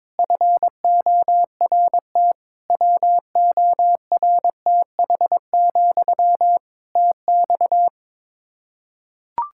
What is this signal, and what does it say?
Morse code, heard as a keyed tone, about 22 wpm, 700 Hz. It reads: FORT WORTH, TX
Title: Practice QSO - Exchange 1 - Beginning - 22wpm